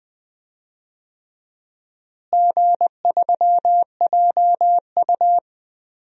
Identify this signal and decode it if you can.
Morse code, heard as a keyed tone, about 20 wpm, 700 Hz. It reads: G3JU
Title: DXCC-100 Call Signs - 4 Characters - Repeated 3x - 20wpm